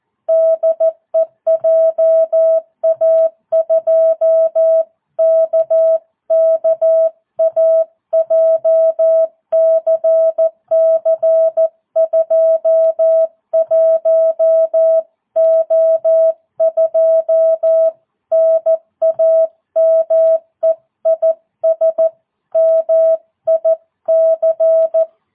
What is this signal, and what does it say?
Morse code, heard as a keyed tone, about 14 wpm, 650 Hz. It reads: DE JA2KKA JCC 21O2 NAME IS MIC